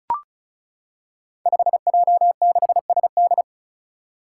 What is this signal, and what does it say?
Morse code, heard as a keyed tone, about 36 wpm, 700 Hz. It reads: 5J6SD
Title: Five Challenging Letter-Number Combinations 35wpm